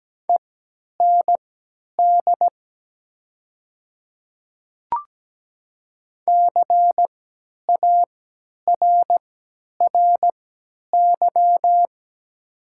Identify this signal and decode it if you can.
Morse code, heard as a keyed tone, about 17 wpm, spaced as for 10 wpm, 700 Hz. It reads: END CARRY
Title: 3x Character Spacing - Top 300 Words - 17wpm